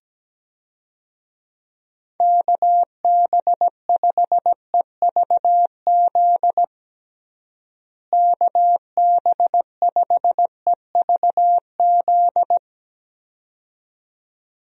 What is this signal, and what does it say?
Morse code, heard as a keyed tone, about 17 wpm, 700 Hz. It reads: KB5EVZ KB5EVZ
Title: USA Call Signs - 6 Characters - Repeated 2x - 17wpm